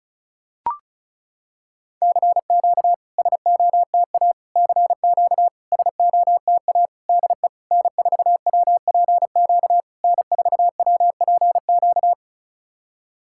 Morse code, heard as a keyed tone, about 36 wpm, 690 Hz. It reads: CQ SOTA CQ SOTA DE N4WPQ N4WPQ
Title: SOTA QSO Practice 35wpm